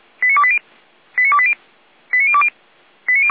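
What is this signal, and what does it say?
Five-tone selective call (ZVEI), heard as quick runs of five tones, tones of about 70 ms. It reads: 89280 89280 89020
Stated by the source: ZVEI five-tone selective call received on SDR.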